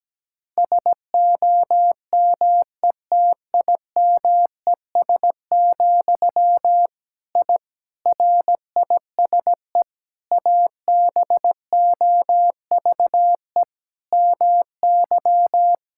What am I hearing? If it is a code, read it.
Morse code, heard as a keyed tone, about 17 wpm, 700 Hz. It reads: SOMETIMES, I RISE ABOVE MY
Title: Sentences from Top 1000 Words - Review Part 6 - 17wpm